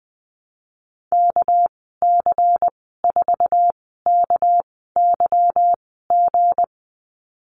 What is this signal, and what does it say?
Morse code, heard as a keyed tone, about 20 wpm, spaced as for 15 wpm, 700 Hz. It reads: KC4KYG
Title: 219 Lesson 35g - Number 0 - Call Signs 20wpm